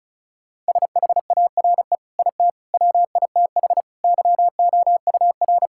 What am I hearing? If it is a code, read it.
Morse code, heard as a keyed tone, about 36 wpm, 700 Hz. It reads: SHARE IT WITH YOUR
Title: Sentences from Top 1000 Words - Review Part 6 - 35wpm